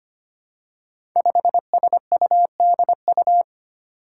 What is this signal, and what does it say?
Morse code, heard as a keyed tone, about 25 wpm, 700 Hz. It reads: 5SUDU
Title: Five Challenging Letter-Number Combinations 25wpm